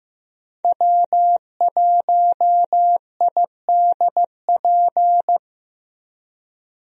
Morse code, heard as a keyed tone, about 15 wpm, 700 Hz. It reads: W1IDP
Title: USA Call Signs - 5 Characters - 15wpm